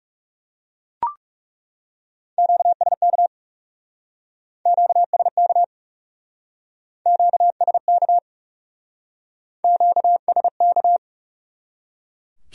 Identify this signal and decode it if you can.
Morse code, heard as a keyed tone, about 38 wpm, 700 Hz. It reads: QSK QSK QSK QSK